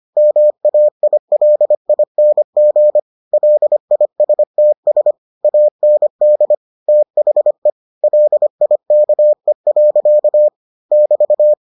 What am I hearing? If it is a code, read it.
Morse code, heard as a keyed tone, about 25 wpm, 600 Hz. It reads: MAILING LISTS AND THE LIKE. <BT>